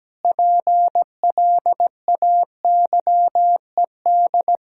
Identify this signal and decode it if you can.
Morse code, heard as a keyed tone, about 17 wpm, 700 Hz. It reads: PLAYED
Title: Pangram Challenge v2 - 17wpm